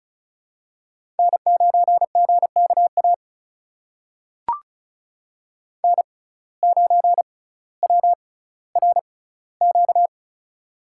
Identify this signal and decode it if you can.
Morse code, heard as a keyed tone, about 36 wpm, spaced as for 18 wpm, 700 Hz. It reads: N9GKA N9WRQ
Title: USA Call Signs - 5 Characters - 6x Character Spacing - 35wpm